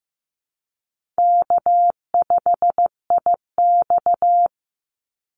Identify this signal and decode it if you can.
Morse code, heard as a keyed tone, about 15 wpm, 700 Hz. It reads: K5IX
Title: Easy Call Signs - 15wpm